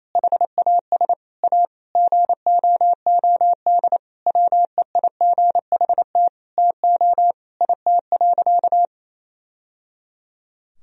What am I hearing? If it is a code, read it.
Morse code, heard as a keyed tone, about 28 wpm, 700 Hz. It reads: HAS A GOOD WEIGHT TO IT.
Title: Sentences from Top 1000 Words - Review Part 3 - 28wpm